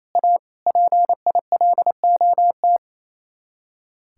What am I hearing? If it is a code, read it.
Morse code, heard as a keyed tone, about 34 wpm, 700 Hz. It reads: A PILOT
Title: Speed-Racing Common QSO Phrases - 28wpm